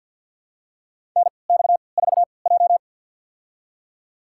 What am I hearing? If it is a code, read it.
Morse code, heard as a keyed tone, about 52 wpm, spaced as for 29 wpm, 700 Hz. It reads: NX4J
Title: USA Call Signs - 4 Characters - 3x Character Spacing - 50wpm